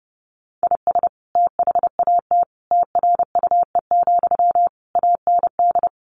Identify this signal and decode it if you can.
Morse code, heard as a keyed tone, about 30 wpm, 700 Hz. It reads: IS THAT TRUE, AND
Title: Sentences from Top 1200 Words 30wpm